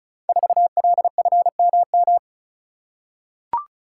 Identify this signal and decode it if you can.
Morse code, heard as a keyed tone, about 36 wpm, 700 Hz. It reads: 4LFMM
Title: Code Groups - 5 Letter:Numbers - 35wpm